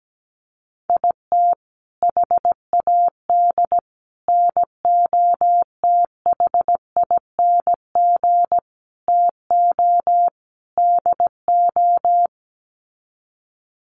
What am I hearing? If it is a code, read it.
Morse code, heard as a keyed tone, about 17 wpm, 700 Hz. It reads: IT HAD NOTHING TO DO
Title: Sets of 5 Words Encore2 17wpm